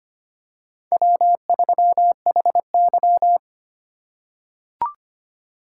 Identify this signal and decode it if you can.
Morse code, heard as a keyed tone, about 25 wpm, 700 Hz. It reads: W3HY
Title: USA Call Signs - 4 Characters - 2x Character Spacing - 25wpm